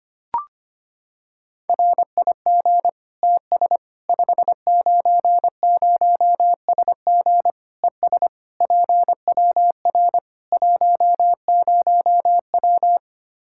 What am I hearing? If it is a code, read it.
Morse code, heard as a keyed tone, about 25 wpm, 700 Hz. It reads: RIG TS 590SG ES PWR 10W
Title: Practice QSO - Exchange 2 - Info - 25wpm